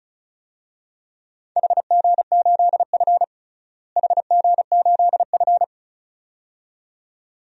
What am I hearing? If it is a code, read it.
Morse code, heard as a keyed tone, about 36 wpm, 700 Hz. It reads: HG8F HG8F
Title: DXCC-100 Call Signs - 4 Characters - Repeated 2x - 35wpm